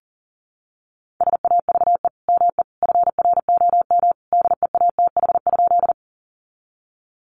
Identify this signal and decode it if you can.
Morse code, heard as a keyed tone, about 40 wpm, 700 Hz. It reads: SAVE ME FROM DEATH?
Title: Sentences from Top 1100 Words 40wpm